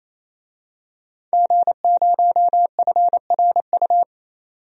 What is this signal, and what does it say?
Morse code, heard as a keyed tone, about 28 wpm, 700 Hz. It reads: G0FRU
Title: DXCC-100 Call Signs - 5 Characters - 28wpm